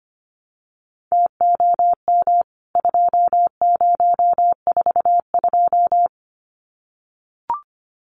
Morse code, heard as a keyed tone, about 25 wpm, 700 Hz. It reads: TOM 2042
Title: CWT Exchange 25wpm